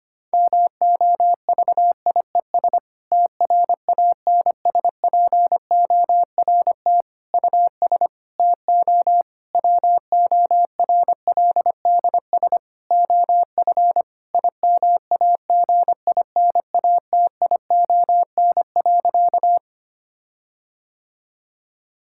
Morse code, heard as a keyed tone, about 25 wpm, 700 Hz. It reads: MOVIES TRANSPORT US TO WORLDS OF IMAGINATION.
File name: practice / 25wpm / Difficult Sentences with Hints - 25wpm